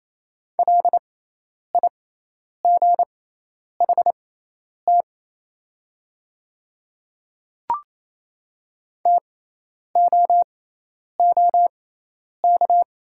Morse code, heard as a keyed tone, about 28 wpm, spaced as for 10 wpm, 700 Hz. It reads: LIGHT TOOK
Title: 6x Character Spacing - Top 300 Words - 28wpm